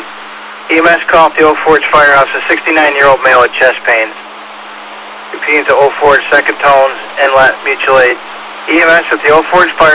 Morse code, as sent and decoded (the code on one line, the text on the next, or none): none